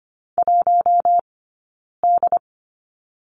Morse code, heard as a keyed tone, about 25 wpm, spaced as for 9 wpm, 690 Hz. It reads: 1D
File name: Lesson 15d - Number 5 - Two Characters 25wpm